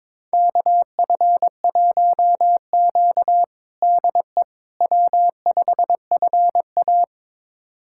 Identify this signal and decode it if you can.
Morse code, heard as a keyed tone, about 22 wpm, 700 Hz. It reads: KF1Q DE W5FA
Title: USA Callsign DE Callsign - 4 Characters - 22wpm